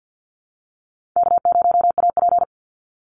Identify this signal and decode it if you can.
Morse code, heard as a keyed tone, about 50 wpm, 700 Hz. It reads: K0AP
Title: Easy Call Signs Repeated 2x - 50wpm